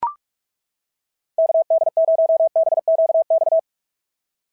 Morse code, heard as a keyed tone, about 41 wpm, 690 Hz. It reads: KD0BQX
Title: SOTA QSO Practice 45wpm